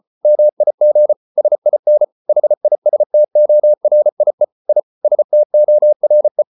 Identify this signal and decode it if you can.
Morse code, heard as a keyed tone, about 34 wpm, 600 Hz. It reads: MIG SIN HISTORIE I STORE
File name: Ferie QRQ 34wpm